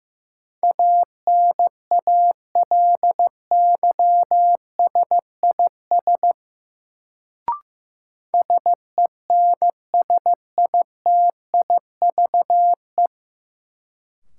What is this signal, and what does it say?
Morse code, heard as a keyed tone, about 16 wpm, 700 Hz. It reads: ANALYSIS SENSITIVE
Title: Sentences from Top 1400 Words - New Word Drill Alt - 15wpm